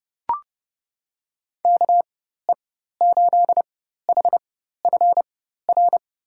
Morse code, heard as a keyed tone, about 32 wpm, spaced as for 14 wpm, 700 Hz. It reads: KE8HFR
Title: USA Call Signs - 6 Characters - 4x Character Spacing - 30wpm